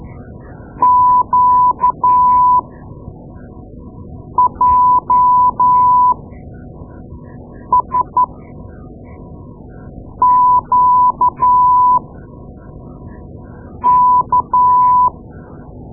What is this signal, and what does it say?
Morse code, heard as a keyed tone, about 12 wpm, spaced as for 4 wpm, 1000 Hz. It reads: QJSQK
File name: Morse Lessons / Morse Lessons / Tape-2A